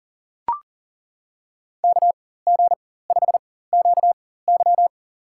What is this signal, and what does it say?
Morse code, heard as a keyed tone, about 38 wpm, spaced as for 18 wpm, 700 Hz. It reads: KG5QY
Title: USA Call Signs - 5 Characters - 4x Character Spacing - 40wpm